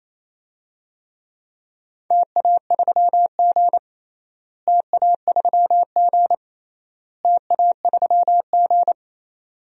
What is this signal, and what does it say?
Morse code, heard as a keyed tone, about 28 wpm, 700 Hz. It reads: TA3G TA3G TA3G
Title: DXCC-100 Call Signs - 4 Characters - Repeated 3x - 28wpm